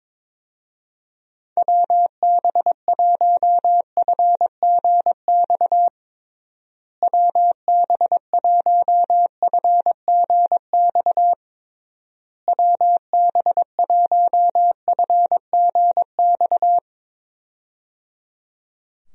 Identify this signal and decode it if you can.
Morse code, heard as a keyed tone, about 22 wpm, 700 Hz. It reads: WB1FGX WB1FGX WB1FGX
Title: USA Call Signs - 6 Characters - Repeated 3x - 22wpm